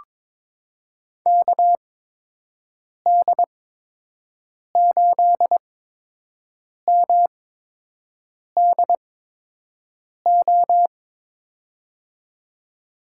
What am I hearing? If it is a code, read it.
Morse code, heard as a keyed tone, about 22 wpm, spaced as for 6 wpm, 700 Hz. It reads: KD8MDO
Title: USA Call Signs - 6 Characters - 8x Character Spacing - 22wpm